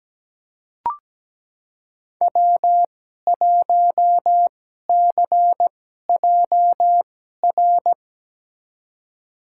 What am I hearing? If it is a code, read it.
Morse code, heard as a keyed tone, about 17 wpm, spaced as for 12 wpm, 700 Hz. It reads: W1CJR
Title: USA Call Signs - 5 Characters - 2x Character Spacing - 17wpm